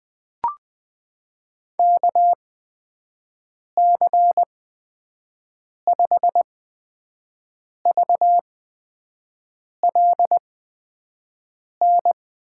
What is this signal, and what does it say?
Morse code, heard as a keyed tone, about 21 wpm, spaced as for 5 wpm, 700 Hz. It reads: KC5VLN
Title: USA Call Signs - 6 Characters - 8x Character Spacing - 20wpm